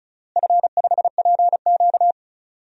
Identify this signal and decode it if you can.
Morse code, heard as a keyed tone, about 34 wpm, 700 Hz. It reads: F5PQ
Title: DXCC-100 Call Signs - 4 Characters - Repeated 3x - 35wpm